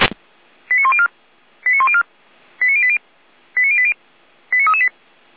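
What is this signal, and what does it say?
Five-tone selective call (ZVEI), heard as quick runs of five tones, tones of about 70 ms. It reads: 89104 89104 89080 89080 89338
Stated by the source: ZVEI five-tone selective call received on SDR.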